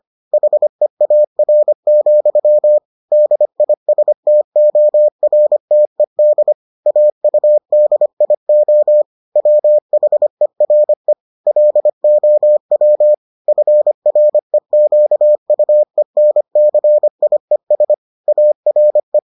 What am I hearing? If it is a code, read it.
Morse code, heard as a keyed tone, about 25 wpm, 600 Hz. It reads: HEAR, DISTORTED AUDIO WHERE LOW FREQUENCIES ARE